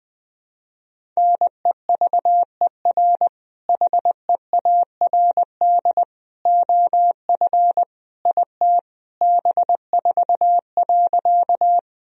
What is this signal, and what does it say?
Morse code, heard as a keyed tone, about 20 wpm, 700 Hz. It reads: NEVER HEARD OF IT B4.
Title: Sentences from Top 400 Words and Abbreviations 20wpm